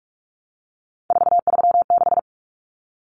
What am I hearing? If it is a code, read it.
Morse code, heard as a keyed tone, about 44 wpm, 700 Hz. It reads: 436
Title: Numbers - 3-Digit 45wpm